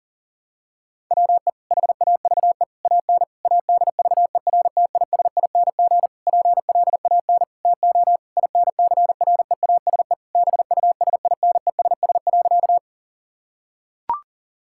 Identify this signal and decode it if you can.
Morse code, heard as a keyed tone, about 40 wpm, 700 Hz. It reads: WE HAVE AN ADVERTISING PLAN TO INCREASE BUSINESS.
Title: Sentences from Top 1000 Words - Review Part 6 - 40wpm